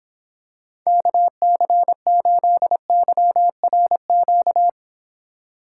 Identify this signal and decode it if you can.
Morse code, heard as a keyed tone, about 26 wpm, 700 Hz. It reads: KC8YRQ